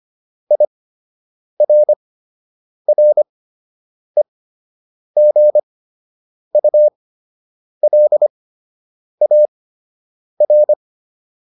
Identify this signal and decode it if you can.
Morse code, heard as a keyed tone, about 25 wpm, spaced as for 8 wpm, 600 Hz. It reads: IRREGULAR